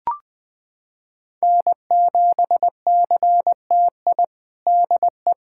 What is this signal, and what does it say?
Morse code, heard as a keyed tone, about 21 wpm, 700 Hz. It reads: N7CTI DE
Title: USA Callsign DE Callsign - 5 Characters - 20wpm